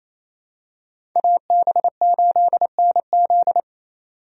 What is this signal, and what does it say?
Morse code, heard as a keyed tone, about 27 wpm, 700 Hz. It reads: AB8NZ
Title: USA Call Signs - 5 Characters - 28wpm